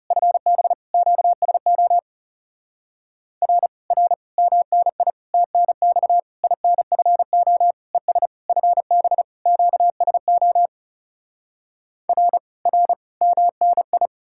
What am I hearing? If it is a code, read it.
Morse code, heard as a keyed tone, about 40 wpm, 700 Hz. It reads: FB QSO R R MNI TNX INFO ES FB QSO R R MNI